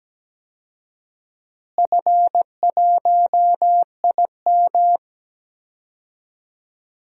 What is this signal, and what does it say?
Morse code, heard as a keyed tone, about 17 wpm, 700 Hz. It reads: F1IM
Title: DXCC-100 Call Signs - 4 Characters - Repeated 3x - 17wpm